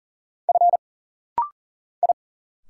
Morse code, heard as a keyed tone, about 40 wpm, 700 Hz. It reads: F I
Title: Single Letters - Rapid-Fire 40wpm